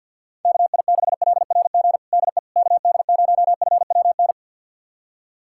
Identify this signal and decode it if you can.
Morse code, heard as a keyed tone, about 53 wpm, 700 Hz. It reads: KI6LRG DE KD0FWN
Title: USA Callsign DE Callsign - 6 Characters - 50wpm